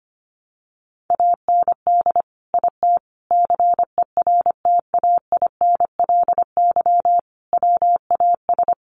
Morse code, heard as a keyed tone, about 25 wpm, 700 Hz. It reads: AND IT CERTAINLY WAS
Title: Sentences from Top 1100 Words 25wpm